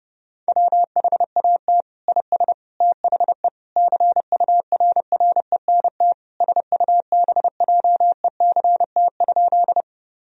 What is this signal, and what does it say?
Morse code, heard as a keyed tone, about 32 wpm, 700 Hz. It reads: WHAT IS THE CURRENT SUBJECT?
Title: Sentences from Top 1000 Words - Review Part 4 - 30wpm